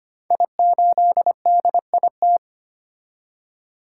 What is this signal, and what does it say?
Morse code, heard as a keyed tone, about 25 wpm, 700 Hz. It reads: I8DIT
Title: Code Groups - 5 Letter:Numbers - 25wpm